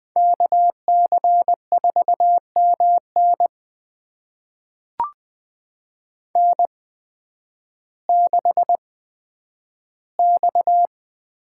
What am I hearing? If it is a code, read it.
Morse code, heard as a keyed tone, about 20 wpm, spaced as for 11 wpm, 700 Hz. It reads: KC4MN N6X
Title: USA Call Signs - 8x Character Spacing - 20wpm